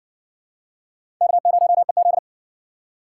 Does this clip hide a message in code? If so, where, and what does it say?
Yes: Morse code from 1.2 to 2.2 s, heard as a keyed tone, about 52 wpm, 700 Hz. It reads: K0EZ